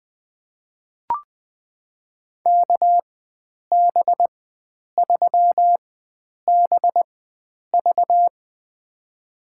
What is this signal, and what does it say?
Morse code, heard as a keyed tone, about 20 wpm, spaced as for 9 wpm, 700 Hz. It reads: KB3BV
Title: USA Call Signs - 5 Characters - 4x Character Spacing - 20wpm